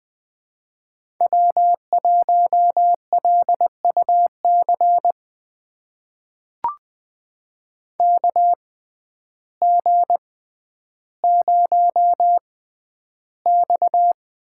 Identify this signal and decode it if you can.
Morse code, heard as a keyed tone, about 20 wpm, spaced as for 11 wpm, 700 Hz. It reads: W1LUC KG0X
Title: USA Call Signs - 5 Characters - 6x Character Spacing - 20wpm